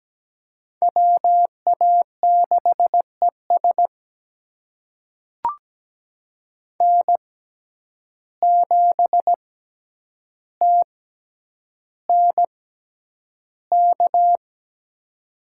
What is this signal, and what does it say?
Morse code, heard as a keyed tone, about 17 wpm, spaced as for 9 wpm, 700 Hz. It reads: WA6ES N7TNK
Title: USA Call Signs - 6x Character Spacing - 17wpm